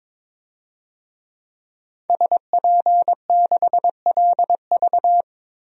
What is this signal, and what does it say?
Morse code, heard as a keyed tone, about 22 wpm, 700 Hz. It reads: SP6LV
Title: DXCC-100 Call Signs - 5 Characters - Repeated 2x - 22wpm